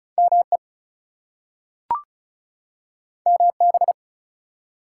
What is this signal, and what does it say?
Morse code, heard as a keyed tone, about 36 wpm, 700 Hz. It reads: ME MB
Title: US States and Canadian Provinces 35wpm